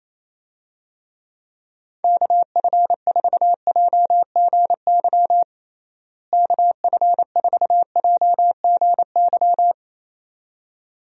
Morse code, heard as a keyed tone, about 28 wpm, 700 Hz. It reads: KF4JGY KF4JGY
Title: USA Call Signs - 6 Characters - Repeated 2x - 28wpm